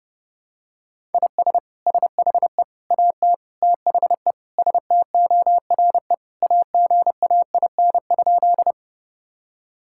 Morse code, heard as a keyed tone, about 32 wpm, 700 Hz. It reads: IS SHE AT THE STORE AGAIN?
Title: Sentences from Top 1000 Words - Review Part 5 - 30wpm